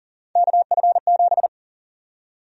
Morse code, heard as a keyed tone, about 38 wpm, 700 Hz. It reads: KF7
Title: USA Call Sign Prefixes - 40wpm